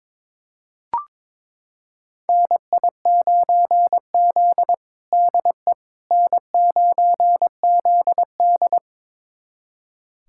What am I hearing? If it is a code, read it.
Morse code, heard as a keyed tone, about 22 wpm, 700 Hz. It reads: NI9Z DE N9ZD
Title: USA Callsign DE Callsign - 4 Characters - 22wpm